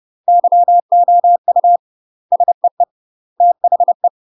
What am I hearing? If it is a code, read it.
Morse code, heard as a keyed tone, about 30 wpm, 700 Hz. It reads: YOU SEE THE
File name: UK MASTER QSOs TRAINING / 30WPM / UK MASTER QSOs 0040